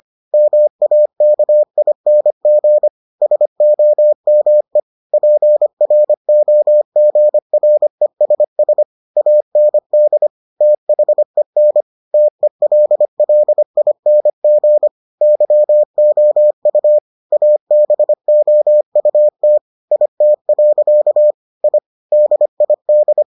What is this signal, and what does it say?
Morse code, heard as a keyed tone, about 25 wpm, 600 Hz. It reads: MAKING SOME PROGRESS AND THEN TELLING YOU ABOUT IT. I DID